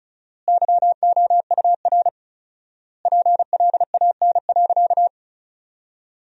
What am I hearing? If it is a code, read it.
Morse code, heard as a keyed tone, about 34 wpm, 700 Hz. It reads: YOUR PLAN.
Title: 4x Word-Spacing - Sentences from Top 500 Words - 35wpm